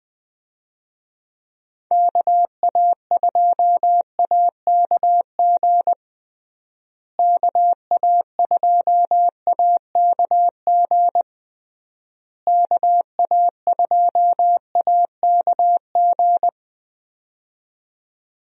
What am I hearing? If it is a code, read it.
Morse code, heard as a keyed tone, about 21 wpm, 700 Hz. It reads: KA2AKG KA2AKG KA2AKG
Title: USA Call Signs - 6 Characters - Repeated 3x - 20wpm